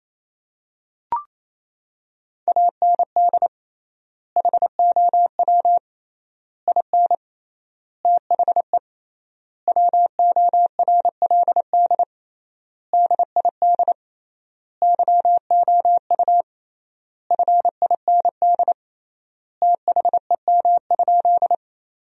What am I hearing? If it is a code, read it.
Morse code, heard as a keyed tone, about 28 wpm, 700 Hz. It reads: AND HOW IN THE WORLD DID YOU FIND THEM?
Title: 3x Word-Spacing - Sentences from Top 200 Words - 28wpm